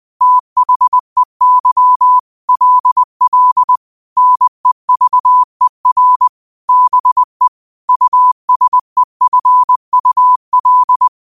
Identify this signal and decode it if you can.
Morse code, heard as a keyed tone, about 20 wpm, 1000 Hz. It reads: THEY LL NEVER BE USEFUL
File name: qotdfiles / 20wpm